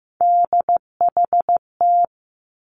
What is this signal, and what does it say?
Morse code, heard as a keyed tone, about 15 wpm, 700 Hz. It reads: DHT
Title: Three Challenging Letter-Number Combinations 15wpm